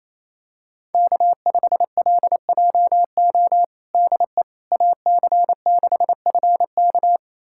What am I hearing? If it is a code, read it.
Morse code, heard as a keyed tone, about 28 wpm, 700 Hz. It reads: K5LJO DE AC6FK
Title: USA Callsign DE Callsign - 5 Characters - 28wpm